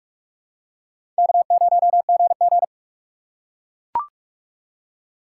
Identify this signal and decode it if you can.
Morse code, heard as a keyed tone, about 43 wpm, 700 Hz. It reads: K0GG
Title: USA Call Signs - 4 Characters - 6x Character Spacing - 45wpm